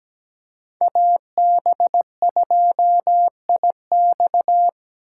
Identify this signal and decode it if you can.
Morse code, heard as a keyed tone, about 20 wpm, 700 Hz. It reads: AB2IX
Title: Speed Racing - USA 2x2 Call Signs 17wpm